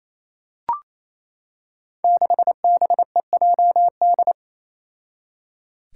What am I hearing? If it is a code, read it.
Morse code, heard as a keyed tone, about 27 wpm, 700 Hz. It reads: 6BEJD
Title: Five Challenging Letter-Number Combinations 28wpm